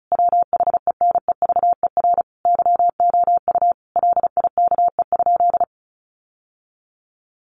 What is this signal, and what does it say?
Morse code, heard as a keyed tone, about 34 wpm, 700 Hz. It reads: WHENEVER YOU LIKE?
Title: Sentences from Top 1200 Words 35wpm